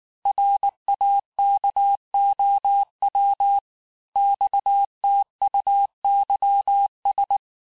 Morse code, heard as a keyed tone, about 19 wpm, 800 Hz. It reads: RAKOW XTUYS
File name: cours-du-mercredi / audios / cours-869-texte-CHIFFRE-vitesse-18-MPM